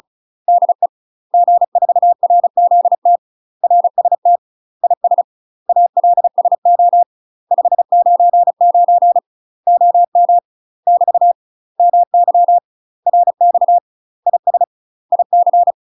Morse code, heard as a keyed tone, about 36 wpm, 700 Hz. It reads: DE G4RZT RST IS ALSO 599 OM <BT> MY RX IS IC
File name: UK MASTER QSOs TRAINING / 35WPM / UK MASTER QSOs 0026